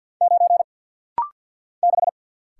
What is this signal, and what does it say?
Morse code, heard as a keyed tone, about 47 wpm, 700 Hz. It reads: 9 6
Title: Single Letter-Number - Rapid-Fire 50wpm